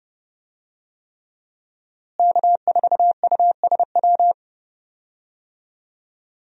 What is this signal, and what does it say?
Morse code, heard as a keyed tone, about 31 wpm, 700 Hz. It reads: K4USW
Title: USA Call Signs - 5 Characters - Repeated 3x - 30wpm